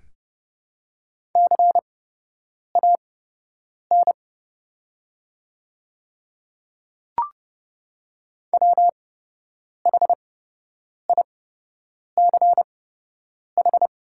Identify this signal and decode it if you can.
Morse code, heard as a keyed tone, about 32 wpm, spaced as for 8 wpm, 700 Hz. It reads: CAN WHICH
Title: 8x Character Spacing - Top 100 Words - 30wpm